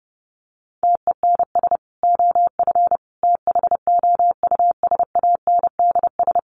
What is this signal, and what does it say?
Morse code, heard as a keyed tone, about 30 wpm, 700 Hz. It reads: TENS OF THOUSANDS
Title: Sets of 5 Words Encore2 30wpm